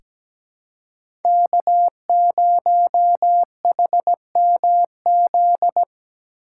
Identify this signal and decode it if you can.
Morse code, heard as a keyed tone, about 17 wpm, 700 Hz. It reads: K0HMZ
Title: K1USN SST QSO Practice - 17wpm